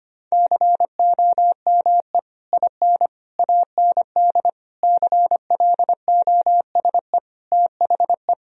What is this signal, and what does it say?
Morse code, heard as a keyed tone, about 25 wpm, 700 Hz. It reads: COME IN AND CLOSE THE
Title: Sentences from Top 1000 Words - Review Part 1 - 25wpm